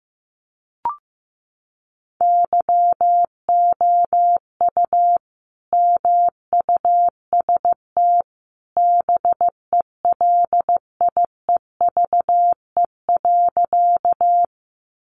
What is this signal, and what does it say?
Morse code, heard as a keyed tone, about 15 wpm, 700 Hz. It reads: YOU MUST BELIEVE.
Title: Sentences from Top 1000 Words 15wpm